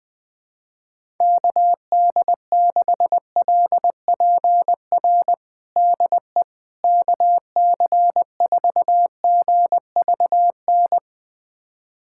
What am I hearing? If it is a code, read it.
Morse code, heard as a keyed tone, about 20 wpm, 700 Hz. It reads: KD6LPR DE KC4GVN
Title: USA Callsign DE Callsign - 6 Characters - 20wpm